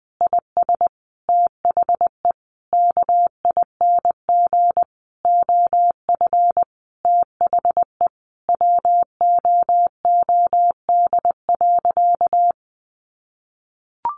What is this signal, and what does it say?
Morse code, heard as a keyed tone, about 20 wpm, 700 Hz. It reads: IS THE KING OF THE WOOD.
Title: Sentences from Top 1200 Words 20wpm